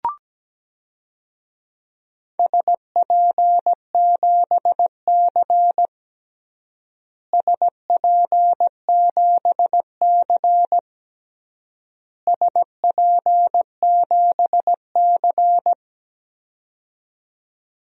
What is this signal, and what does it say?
Morse code, heard as a keyed tone, about 17 wpm, 700 Hz. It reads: SP7C SP7C SP7C
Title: DXCC-100 Call Signs - 4 Characters - Repeated 3x - 17wpm